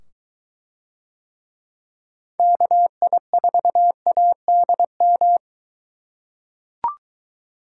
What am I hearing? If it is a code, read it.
Morse code, heard as a keyed tone, about 23 wpm, 700 Hz. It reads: KI4ADM